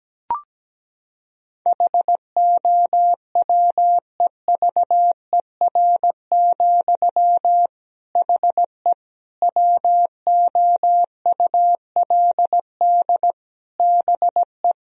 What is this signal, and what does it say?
Morse code, heard as a keyed tone, about 17 wpm, 700 Hz. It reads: HOWEVER, HE WOULD BE
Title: Sentences from Top 1000 Words - Review Part 2 - 17wpm